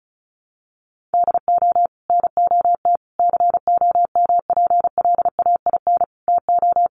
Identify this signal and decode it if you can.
Morse code, heard as a keyed tone, about 34 wpm, 700 Hz. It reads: DO NOT COMPLAIN TO
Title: Sentences from Top 1300 Words 35wpm